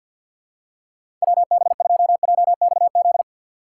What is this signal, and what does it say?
Morse code, heard as a keyed tone, about 53 wpm, 700 Hz. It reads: WB2JXZ
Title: USA Call Signs - 6 Characters - 4x Character Spacing - 50wpm